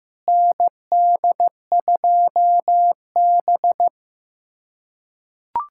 Morse code, heard as a keyed tone, about 15 wpm, 700 Hz. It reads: ND2B
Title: USA Callsign DE Callsign - 4 Characters - 15wpm